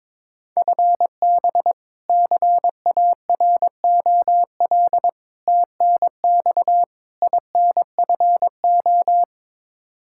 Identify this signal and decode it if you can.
Morse code, heard as a keyed tone, about 22 wpm, 700 Hz. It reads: FB CAROL TNX INFO
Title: K1USN SST QSO Practice 22wpm